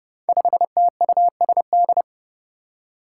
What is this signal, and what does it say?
Morse code, heard as a keyed tone, about 30 wpm, 700 Hz. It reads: 5TUSD
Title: Five Challenging Letter-Number Combinations 30wpm